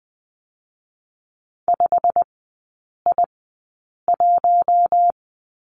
Morse code, heard as a keyed tone, about 20 wpm, spaced as for 8 wpm, 700 Hz. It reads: 5I1
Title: 087 Lesson 15e - Number 5 - Three Characters 20wpm